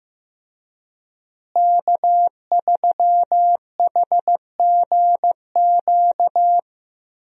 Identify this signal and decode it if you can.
Morse code, heard as a keyed tone, about 15 wpm, 700 Hz. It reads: K3HGQ
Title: USA Call Signs - 5 Characters - Repeated 3x - 15wpm